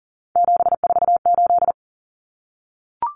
Morse code, heard as a keyed tone, about 40 wpm, 700 Hz. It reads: 748
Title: Common Cut Numbers - 3-Digit 40wpm